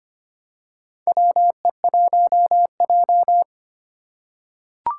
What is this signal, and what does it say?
Morse code, heard as a keyed tone, about 25 wpm, 700 Hz. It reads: WE1J
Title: USA Call Signs - 2x Character Spacing - 25wpm